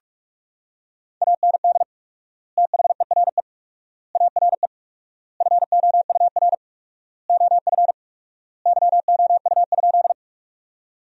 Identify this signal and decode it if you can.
Morse code, heard as a keyed tone, about 46 wpm, 700 Hz. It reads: AND THERE ARE FOUR OF YOU?
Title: 4x Word-Spacing - Sentences from Top 300 Words - 45wpm